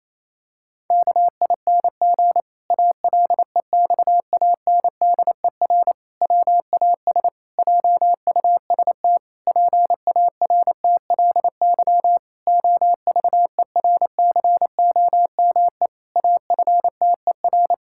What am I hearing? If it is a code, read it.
Morse code, heard as a keyed tone, about 27 wpm, 700 Hz. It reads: KING ALEXANDER WAS JUST PARTLY OVERCOME AFTER
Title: Pangram Challenge v2 - 28wpm